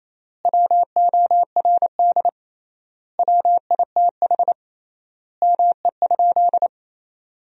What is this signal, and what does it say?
Morse code, heard as a keyed tone, about 28 wpm, 700 Hz. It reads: WORD WITH ME?
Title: 3x Word-Spacing - Sentences from Top 300 Words - 28wpm